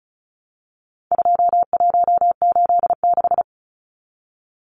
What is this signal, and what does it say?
Morse code, heard as a keyed tone, about 34 wpm, 700 Hz. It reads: 2186
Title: Numbers - 4-Digit 35wpm